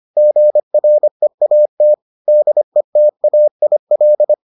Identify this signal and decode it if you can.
Morse code, heard as a keyed tone, about 25 wpm, 600 Hz. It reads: GREAT DETAIL